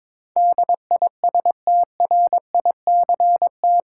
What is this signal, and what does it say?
Morse code, heard as a keyed tone, about 22 wpm, 700 Hz. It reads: DISTRICT
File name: practice / 22wpm / Eight Letter Words - 22wpm